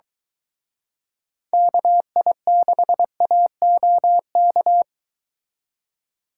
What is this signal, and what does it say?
Morse code, heard as a keyed tone, about 22 wpm, 700 Hz. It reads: KI6AOK